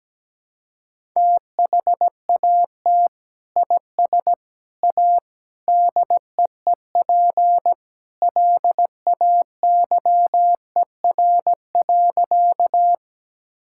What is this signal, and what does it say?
Morse code, heard as a keyed tone, about 17 wpm, 700 Hz. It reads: THAT IS A DEEP LAYER.
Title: Sentences from Top 1600 Words 17wpm